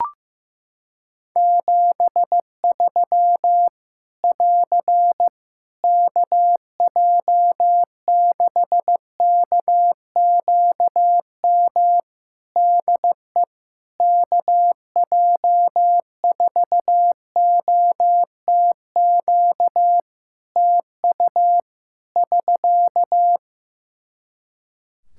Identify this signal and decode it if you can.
Morse code, heard as a keyed tone, about 15 wpm, 700 Hz. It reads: 73 <AR> KJ6KQM DE KJ4OTQ TU <SK>
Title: Practice QSO - Exchange 4 - Ending - 15wpm